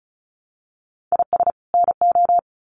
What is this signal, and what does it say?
Morse code, heard as a keyed tone, about 34 wpm, 700 Hz. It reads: IS NO
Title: Sets of 2 Words 35wpm